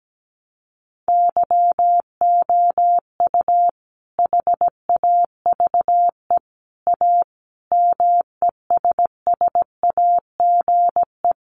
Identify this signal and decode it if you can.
Morse code, heard as a keyed tone, about 17 wpm, 700 Hz. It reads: YOU HAVE A MESSAGE
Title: Sentences from Top 1100 Words 17wpm